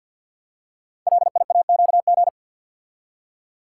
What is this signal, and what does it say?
Morse code, heard as a keyed tone, about 52 wpm, 700 Hz. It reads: LIAQG
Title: Code Groups - 5 Letters - 50wpm